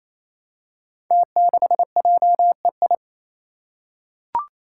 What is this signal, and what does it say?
Morse code, heard as a keyed tone, about 28 wpm, 700 Hz. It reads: T6JEI
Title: Five Challenging Letter-Number Combinations 28wpm